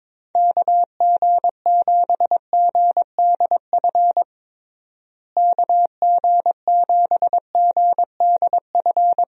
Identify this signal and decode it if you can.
Morse code, heard as a keyed tone, about 21 wpm, 700 Hz. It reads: KG7GDF KG7GDF
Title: USA Call Signs - 6 Characters - Repeated 3x - 22wpm